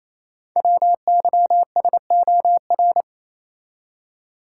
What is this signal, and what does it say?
Morse code, heard as a keyed tone, about 29 wpm, 700 Hz. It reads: WYSOR
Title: Code Groups - 5 Letters - 28wpm